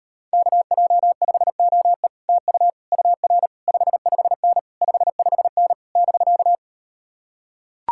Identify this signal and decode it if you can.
Morse code, heard as a keyed tone, about 38 wpm, 700 Hz. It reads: KJ5OE TU UR 55N 55N <BK>